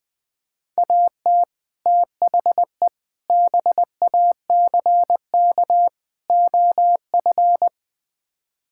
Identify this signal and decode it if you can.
Morse code, heard as a keyed tone, about 20 wpm, 700 Hz. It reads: AT THE BACK OF
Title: Build A Phrase - 5-Word Phrases - 20wpm